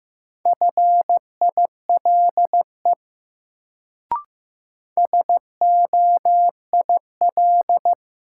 Morse code, heard as a keyed tone, about 15 wpm, 700 Hz. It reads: FILE SOIL
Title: Sentences from Top 1000 Words - Review Part 4 - New Word Drill - 15wpm